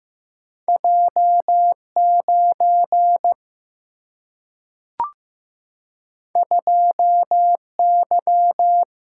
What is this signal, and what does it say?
Morse code, heard as a keyed tone, about 15 wpm, 700 Hz. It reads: J 9 2 Y
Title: Single-Letter Single-Number 15wpm